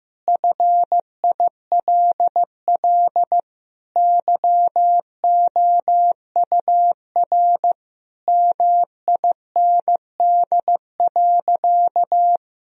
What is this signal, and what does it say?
Morse code, heard as a keyed tone, about 15 wpm, 700 Hz. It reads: FILL YOUR MIND.
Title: Sentences from Top 1000 Words - Review Part 3 - 15wpm